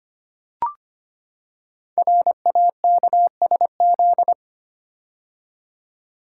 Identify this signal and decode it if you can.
Morse code, heard as a keyed tone, about 25 wpm, 700 Hz. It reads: RAKSZ
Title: Code Groups - 5 Letter:Numbers - 25wpm